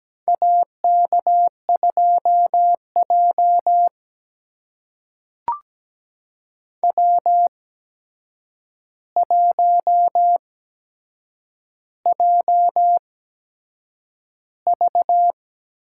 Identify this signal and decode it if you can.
Morse code, heard as a keyed tone, about 17 wpm, spaced as for 7 wpm, 700 Hz. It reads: AK2J W1JV
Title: USA Call Signs - 4 Characters - 8x Character Spacing - 17wpm